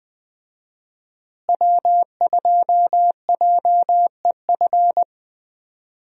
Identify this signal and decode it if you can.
Morse code, heard as a keyed tone, about 20 wpm, 700 Hz. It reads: W2JEF
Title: USA Call Signs - 5 Characters - Repeated 2x - 20wpm